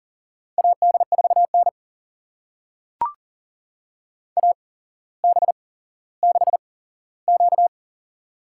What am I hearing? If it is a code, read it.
Morse code, heard as a keyed tone, about 42 wpm, spaced as for 17 wpm, 700 Hz. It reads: AD4N AB6Q
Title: USA Call Signs - 4 Characters - 8x Character Spacing - 40wpm